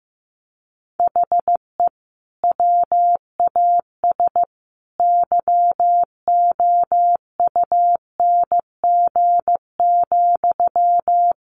Morse code, heard as a keyed tone, about 15 wpm, 700 Hz. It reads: HE WAS YOUNG,
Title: Sentences from Top 1100 Words 15wpm